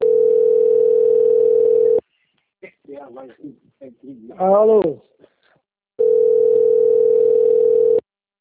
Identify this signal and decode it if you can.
Ringback tone (North America), 440 + 480 Hz, with speech.
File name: out- - - - - .28763